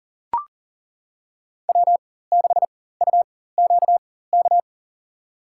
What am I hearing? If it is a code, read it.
Morse code, heard as a keyed tone, about 40 wpm, spaced as for 19 wpm, 700 Hz. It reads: W6UQK
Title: USA Call Signs - 4x Character Spacing - 40wpm